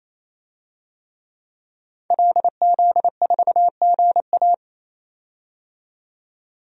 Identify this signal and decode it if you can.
Morse code, heard as a keyed tone, about 28 wpm, 700 Hz. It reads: LZ4GA
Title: DXCC-100 Call Signs - 5 Characters - 28wpm